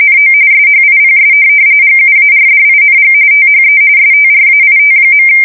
RTTY, about 45.45 baud. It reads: NO SHIPS GUARDING THE STRAIT.
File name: fsk170 / USS-New-Jersey